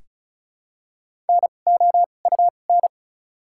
Morse code, heard as a keyed tone, about 36 wpm, 700 Hz. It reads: N O U N
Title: 2x Character Spacing - Top 500 Words - 35wpm